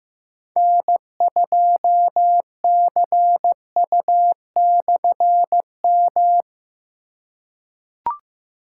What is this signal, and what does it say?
Morse code, heard as a keyed tone, about 16 wpm, 700 Hz. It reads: N2CU/M
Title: K1USN SST QSO Practice - 15wpm